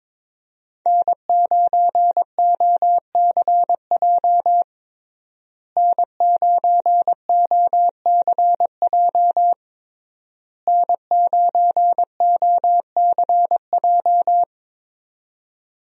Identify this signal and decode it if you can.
Morse code, heard as a keyed tone, about 21 wpm, 700 Hz. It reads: N9OCJ N9OCJ N9OCJ
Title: USA Call Signs - 5 Characters - Repeated 3x - 22wpm